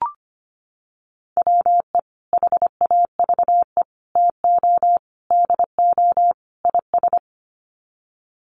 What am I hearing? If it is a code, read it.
Morse code, heard as a keyed tone, about 25 wpm, 700 Hz. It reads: WE HAVE TO DO IS
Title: Sets of 5 Words Encore 25wpm